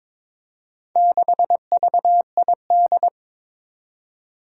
Morse code, heard as a keyed tone, about 22 wpm, 700 Hz. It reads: 6VID
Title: Four Challenging Letter-Number Combinations 22wpm